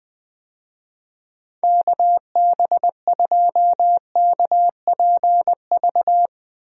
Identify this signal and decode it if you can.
Morse code, heard as a keyed tone, about 20 wpm, 700 Hz. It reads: KB2KPV
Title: USA Call Signs - 6 Characters - Repeated 3x - 20wpm